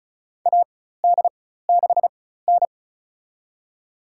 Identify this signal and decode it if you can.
Morse code, heard as a keyed tone, about 34 wpm, spaced as for 16 wpm, 700 Hz. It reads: AD6N
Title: USA Call Signs - 4 Characters - 4x Character Spacing - 35wpm